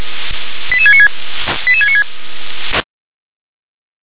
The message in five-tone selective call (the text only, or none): 88696 88696